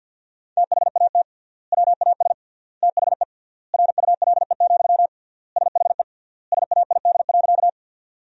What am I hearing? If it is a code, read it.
Morse code, heard as a keyed tone, about 52 wpm, 700 Hz. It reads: THAT WAS THE RULE, SHE SAID.
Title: 3x Word-Spacing - Sentences from Top 400 Words - 50wpm